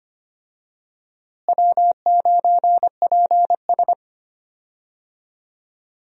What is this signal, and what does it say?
Morse code, heard as a keyed tone, about 25 wpm, 700 Hz. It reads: W9PS
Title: USA Call Signs - 4 Characters - Repeated 3x - 25wpm